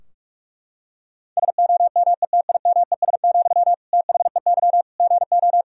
Morse code, heard as a keyed tone, about 41 wpm, 700 Hz. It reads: SOMETIMES, THEY GO
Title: Sentences from Top 1000 Words - Review Part 3 - 45wpm